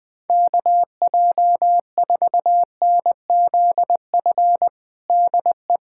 Morse code, heard as a keyed tone, about 20 wpm, 700 Hz. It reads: KJ4NZF DE
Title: USA Callsign DE Callsign - 6 Characters - 20wpm